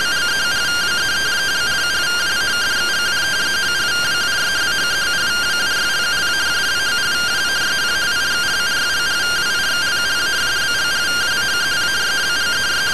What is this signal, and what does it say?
Two-tone teleprinter signal, about 170 Hz shift.